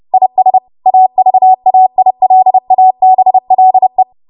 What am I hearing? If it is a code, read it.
Morse code, heard as a keyed tone, about 30 wpm, 750 Hz. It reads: IS AVAILABLE